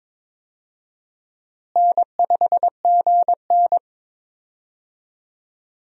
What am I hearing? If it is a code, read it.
Morse code, heard as a keyed tone, about 22 wpm, 700 Hz. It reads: N5GN
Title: USA Call Signs - 4 Characters - Repeated 3x - 22wpm